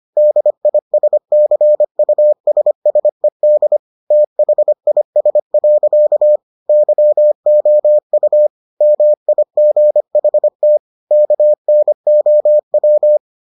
Morse code, heard as a keyed tone, about 25 wpm, 600 Hz. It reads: DISCUSSED THIS. YOU MIGHT KNOW